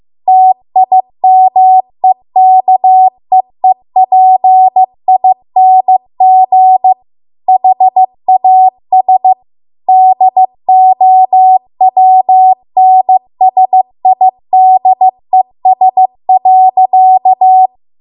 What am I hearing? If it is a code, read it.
Morse code, heard as a keyed tone, about 15 wpm, 750 Hz. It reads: TIMEKEEPING HAS DOWNSIDES.
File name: Archive / 15 WPM / 15WPM